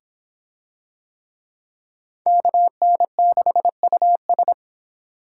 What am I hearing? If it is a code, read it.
Morse code, heard as a keyed tone, about 26 wpm, 700 Hz. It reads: KN6US